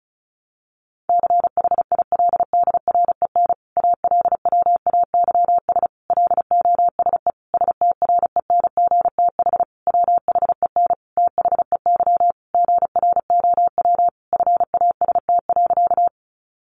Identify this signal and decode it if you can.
Morse code, heard as a keyed tone, about 34 wpm, 700 Hz. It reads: CHILDREN ALWAYS LOSE STRENGTH WHEN THEY GROW FAST.
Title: Sentences from Top 1300 Words 35wpm